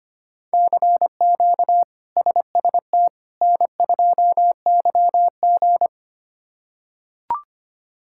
Morse code, heard as a keyed tone, about 25 wpm, 700 Hz. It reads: CQ SST N2YG
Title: K1USN SST QSO Practice - 25wpm